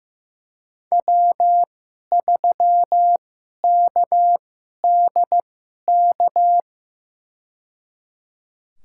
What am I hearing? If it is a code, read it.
Morse code, heard as a keyed tone, about 15 wpm, spaced as for 11 wpm, 700 Hz. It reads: W3KDK
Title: USA Call Signs - 5 Characters - 2x Character Spacing - 15wpm